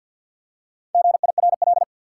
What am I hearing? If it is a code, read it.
Morse code, heard as a keyed tone, about 53 wpm, 700 Hz. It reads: GIRL
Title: Four Letter Words - 50wpm